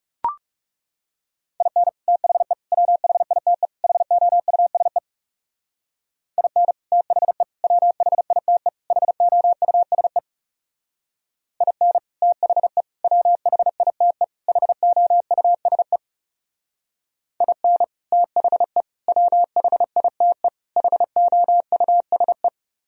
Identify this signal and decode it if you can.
Morse code, heard as a keyed tone, about 37 wpm, 700 Hz. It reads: IN THE WHITE HOUSE IN THE WHITE HOUSE IN THE WHITE HOUSE IN THE WHITE HOUSE
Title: Speed-Racing - Sets of 4 Words 30wpm